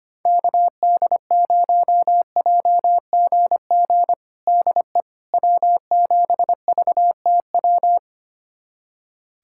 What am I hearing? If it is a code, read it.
Morse code, heard as a keyed tone, about 25 wpm, 700 Hz. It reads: KD0JGG DE W7VTW
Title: USA Callsign DE Callsign 25wpm